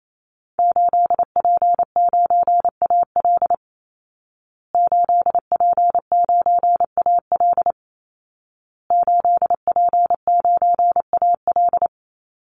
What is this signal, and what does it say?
Morse code, heard as a keyed tone, about 28 wpm, 700 Hz. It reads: 8P9AL 8P9AL 8P9AL
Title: Call Signs Repeated 2x - 28wpm